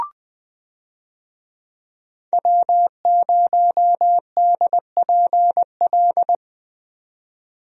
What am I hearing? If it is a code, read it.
Morse code, heard as a keyed tone, about 20 wpm, 700 Hz. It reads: W0DPL
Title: USA Call Signs - 5 Characters - 20wpm